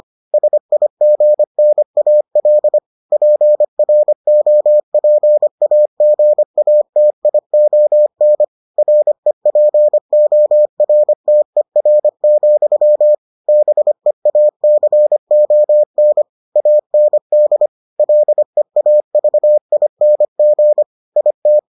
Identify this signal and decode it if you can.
Morse code, heard as a keyed tone, about 25 wpm, 600 Hz. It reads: SIGNAL PROPAGATION REPORTER, BEACON AND LEAVING IT